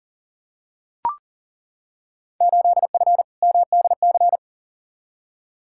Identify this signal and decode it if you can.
Morse code, heard as a keyed tone, about 40 wpm, 700 Hz. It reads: 8F MDC
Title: Field Day Exchanges Encore - 40wpm